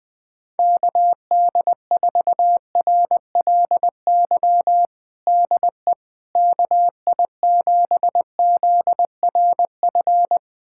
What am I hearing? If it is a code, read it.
Morse code, heard as a keyed tone, about 20 wpm, 700 Hz. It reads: KD4RLY DE KI7ZRF
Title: USA Callsign DE Callsign 20wpm